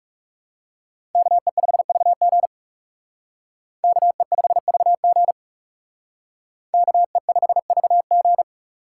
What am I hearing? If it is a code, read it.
Morse code, heard as a keyed tone, about 38 wpm, 700 Hz. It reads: KE5VG KE5VG KE5VG